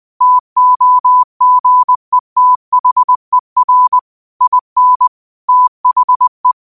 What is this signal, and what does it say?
Morse code, heard as a keyed tone, about 20 wpm, 1000 Hz. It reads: TOGETHER IN THE